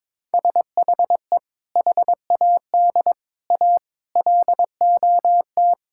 Morse code, heard as a keyed tone, about 22 wpm, 700 Hz. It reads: SHE HAD A LOT
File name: practice / 22wpm / Sentences with 4-Word Phrases - 22wpm